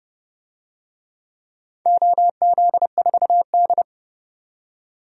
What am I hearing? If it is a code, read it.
Morse code, heard as a keyed tone, about 31 wpm, 700 Hz. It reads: OZ4D
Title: DXCC-100 Call Signs - 4 Characters - 30wpm